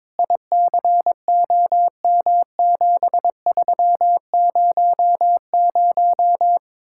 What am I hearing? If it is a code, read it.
Morse code, heard as a keyed tone, about 23 wpm, 700 Hz. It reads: ICOM7300
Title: Speed-Racing Common QSO Phrases - 22wpm